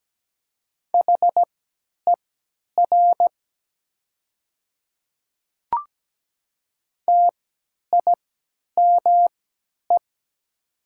Morse code, heard as a keyed tone, about 17 wpm, spaced as for 10 wpm, 700 Hz. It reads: HER TIME
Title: 3x Character Spacing - Top 100 Words - 17wpm